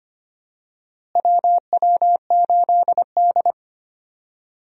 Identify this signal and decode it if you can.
Morse code, heard as a keyed tone, about 25 wpm, 700 Hz. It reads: WW8D
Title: USA Call Signs - 4 Characters - Repeated 2x - 25wpm